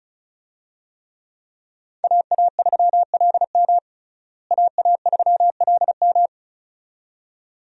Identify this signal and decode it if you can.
Morse code, heard as a keyed tone, about 36 wpm, 700 Hz. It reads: AA3LM AA3LM
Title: USA Call Signs - 5 Characters - Repeated 2x - 35wpm